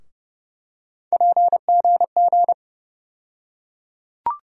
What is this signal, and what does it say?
Morse code, heard as a keyed tone, about 30 wpm, 700 Hz. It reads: PGG
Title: USA Call Sign Postfixes - 3 Letters 30wpm